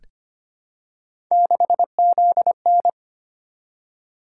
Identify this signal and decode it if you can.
Morse code, heard as a keyed tone, about 25 wpm, 700 Hz. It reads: 6ZN
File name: Three Letter-Number Combinations - 25wpm